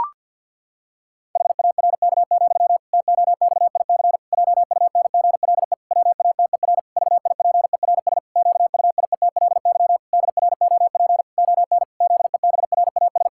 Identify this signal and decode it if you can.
Morse code, heard as a keyed tone, about 50 wpm, 700 Hz. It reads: HARK, TOXIC JUNGLE WATER VIPERS QUIETLY DROP ON ZEBRAS
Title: Pangram Challenge v2 - 50wpm